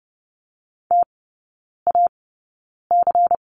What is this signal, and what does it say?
Morse code, heard as a keyed tone, about 30 wpm, spaced as for 9 wpm, 700 Hz. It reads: TAC
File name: Lesson 16e - Letter c - Three Characters 30wpm